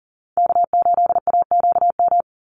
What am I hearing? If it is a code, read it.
Morse code, heard as a keyed tone, about 40 wpm, 700 Hz. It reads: K8AQM
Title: Call Signs Repeated 3x - 40wpm